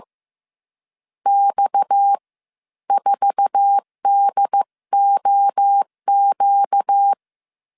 Morse code, heard as a keyed tone, about 15 wpm, spaced as for 12 wpm, 800 Hz. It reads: X4DOQ
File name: CW / DET / 1 9 1500C S C 14wpm 3245